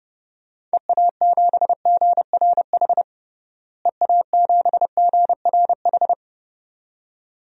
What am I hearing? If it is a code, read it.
Morse code, heard as a keyed tone, about 30 wpm, 700 Hz. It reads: EA7GRH EA7GRH
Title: DXCC-100 Call Signs - 6 Characters - Repeated 2x - 30wpm